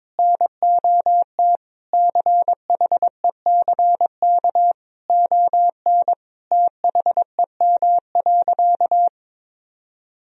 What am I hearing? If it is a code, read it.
Morse code, heard as a keyed tone, about 22 wpm, 700 Hz. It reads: NOT CHECK ON THEM.
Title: Sentences from Top 1000 Words - Review Part 3 - 22wpm